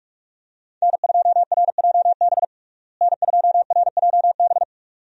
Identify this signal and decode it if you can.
Morse code, heard as a keyed tone, about 46 wpm, 700 Hz. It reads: N2RJB N2RJB
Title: USA Call Signs - 5 Characters - Repeated 2x - 45wpm